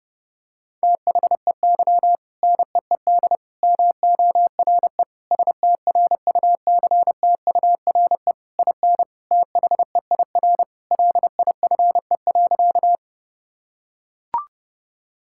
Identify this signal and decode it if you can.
Morse code, heard as a keyed tone, about 30 wpm, 700 Hz. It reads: THEY NEED MORE STRUCTURE IN THEIR LIFE.
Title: Sentences from Top 1000 Words - Review Part 6 - 30wpm